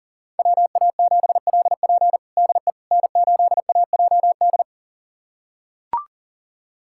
Morse code, heard as a keyed tone, about 40 wpm, 700 Hz. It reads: WA7LP DE N8AJD
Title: USA Callsign DE Callsign - 5 Characters - 40wpm